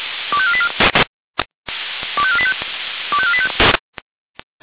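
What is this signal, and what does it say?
Five-tone selective call (ZVEI), heard as quick runs of five tones, tones of about 80 ms. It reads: 25484 25485 25485
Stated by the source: Signal received via SDR, ZVEI five-tone.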